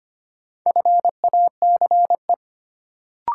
Morse code, heard as a keyed tone, about 25 wpm, 700 Hz. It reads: FACE
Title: Sentences from Top 300 Words - New Word Drill - 25wpm